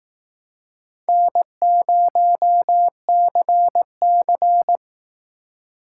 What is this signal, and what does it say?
Morse code, heard as a keyed tone, about 19 wpm, 700 Hz. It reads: N0CC